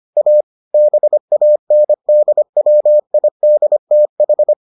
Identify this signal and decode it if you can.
Morse code, heard as a keyed tone, about 25 wpm, 600 Hz. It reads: A BANDWIDTH